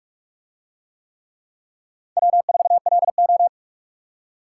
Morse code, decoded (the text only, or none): W4LO